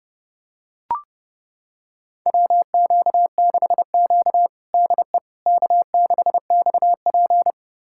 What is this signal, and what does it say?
Morse code, heard as a keyed tone, about 30 wpm, 700 Hz. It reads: WQ6Q DE K6XP
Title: USA Callsign DE Callsign - 4 Characters - 30wpm